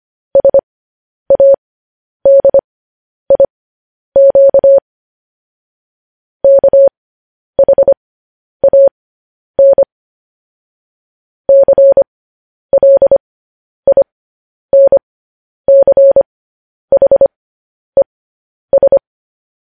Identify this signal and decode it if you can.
Morse code, heard as a keyed tone, about 25 wpm, spaced as for 10 wpm, 550 Hz. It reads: SADIQ KHAN CLINCHES
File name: podcast / News headlines-1x-25x10